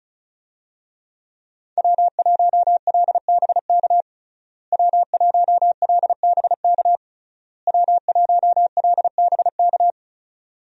Morse code, heard as a keyed tone, about 36 wpm, 700 Hz. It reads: W1LBK W1LBK W1LBK
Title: USA Call Signs - 5 Characters - Repeated 3x - 35wpm